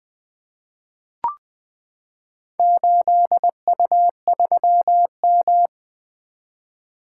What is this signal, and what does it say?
Morse code, heard as a keyed tone, about 20 wpm, 700 Hz. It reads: 8U3M
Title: Four Letter-Number Combinations - 20wpm